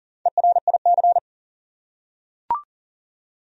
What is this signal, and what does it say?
Morse code, heard as a keyed tone, about 40 wpm, 700 Hz. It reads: ERIC
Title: USA Common Names - 40wpm